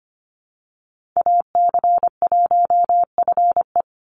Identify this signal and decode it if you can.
Morse code, heard as a keyed tone, about 25 wpm, 700 Hz. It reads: AC1FE
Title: Call Signs - 25wpm